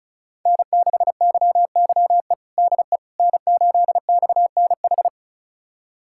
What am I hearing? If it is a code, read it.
Morse code, heard as a keyed tone, about 34 wpm, 700 Hz. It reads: N6YYE DE N8XNH
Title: USA Callsign DE Callsign - 5 Characters - 35wpm